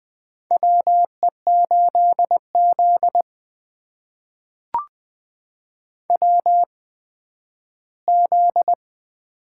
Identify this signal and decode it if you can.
Morse code, heard as a keyed tone, about 20 wpm, spaced as for 12 wpm, 700 Hz. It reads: WE8Z WZ
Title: USA Call Signs - 4 Characters - 8x Character Spacing - 20wpm